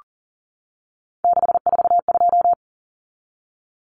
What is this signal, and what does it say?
Morse code, heard as a keyed tone, about 37 wpm, 700 Hz. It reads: 642
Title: Numbers - 3-Digit 40wpm